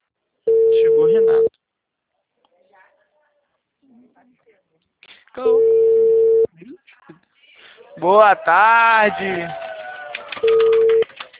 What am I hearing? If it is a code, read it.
Ringback tone (China, 450 Hz), with speech.